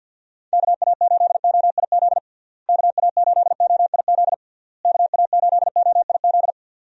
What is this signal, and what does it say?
Morse code, heard as a keyed tone, about 51 wpm, 700 Hz. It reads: KA8OIZ KA8OIZ KA8OIZ
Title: USA Call Signs - 6 Characters - Repeated 3x - 50wpm